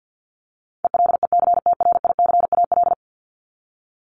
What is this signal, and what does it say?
Morse code, heard as a keyed tone, about 50 wpm, 700 Hz. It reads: ELECTRICAL
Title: Top 751-1000 Words 50wpm